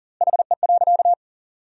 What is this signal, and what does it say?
Morse code, heard as a keyed tone, about 40 wpm, 700 Hz. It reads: HE.
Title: 2x Word-Spacing - Sentences from Top 400 Words - 40wpm